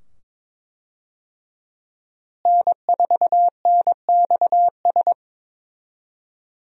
Morse code, heard as a keyed tone, about 22 wpm, 700 Hz. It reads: N4NXS
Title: USA Call Signs - 5 Characters - Repeated 2x - 22wpm